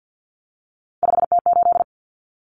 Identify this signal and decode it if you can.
Morse code, heard as a keyed tone, about 50 wpm, 700 Hz. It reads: 5T8
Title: Common Cut Numbers - 3-Digit 50wpm